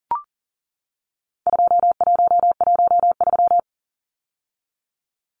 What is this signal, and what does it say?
Morse code, heard as a keyed tone, about 40 wpm, 700 Hz. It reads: 2113
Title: Numbers - 4-Digit 40wpm